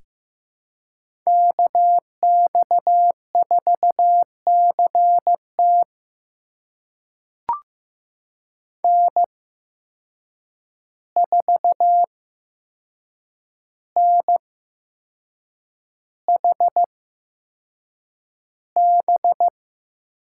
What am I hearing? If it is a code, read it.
Morse code, heard as a keyed tone, about 15 wpm, spaced as for 6 wpm, 700 Hz. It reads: KX4CT N4NHB
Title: USA Call Signs - 5 Characters - 8x Character Spacing - 15wpm